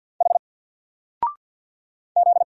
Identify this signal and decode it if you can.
Morse code, heard as a keyed tone, about 50 wpm, 700 Hz. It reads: H Z
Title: Single Letter-Number Alt 50wpm